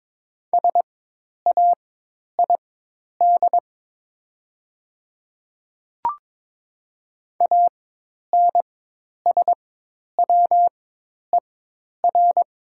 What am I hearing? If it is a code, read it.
Morse code, heard as a keyed tone, about 22 wpm, spaced as for 10 wpm, 700 Hz. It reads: SAID ANSWER
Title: 4x Character Spacing - Top 500 Words - 22wpm